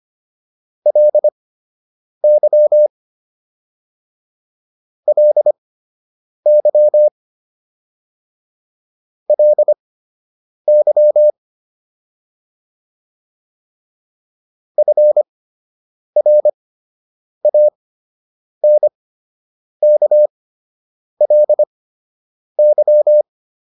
Morse code, heard as a keyed tone, about 25 wpm, spaced as for 8 wpm, 600 Hz. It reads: LY LY LY FRANKLY